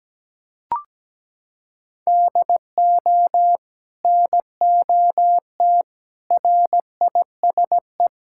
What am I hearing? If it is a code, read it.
Morse code, heard as a keyed tone, about 17 wpm, 700 Hz. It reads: DO NOT RISE
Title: Sentences from Top 1000 Words - Review Part 6 - 17wpm